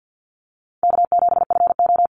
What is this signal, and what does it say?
Morse code, heard as a keyed tone, about 50 wpm, 700 Hz. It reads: K7FO
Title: NA SOTA Chaser Callsigns 50wpm